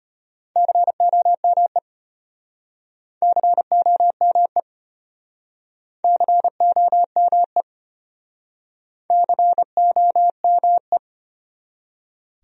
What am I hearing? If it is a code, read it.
Morse code, heard as a keyed tone, about 32 wpm, 700 Hz. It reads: COME COME COME COME